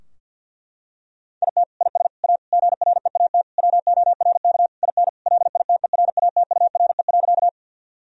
Morse code, heard as a keyed tone, about 50 wpm, 700 Hz. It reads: IT IS A GREAT WORK IN LITERATURE.
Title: Sentences from Top 1000 Words - Review Part 3 - 50wpm